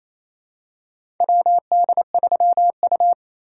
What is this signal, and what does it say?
Morse code, heard as a keyed tone, about 28 wpm, 700 Hz. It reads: WD3U
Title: USA Call Signs - 4 Characters - 2x Character Spacing - 28wpm